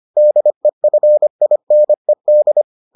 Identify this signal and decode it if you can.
Morse code, heard as a keyed tone, about 25 wpm, 600 Hz. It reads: DEFINED